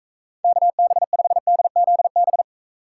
Morse code, heard as a keyed tone, about 39 wpm, 700 Hz. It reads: KB5DZB